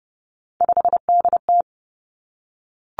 Morse code, heard as a keyed tone, about 30 wpm, 700 Hz. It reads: 5DT
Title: Three Challenging Letter-Number Combinations 30wpm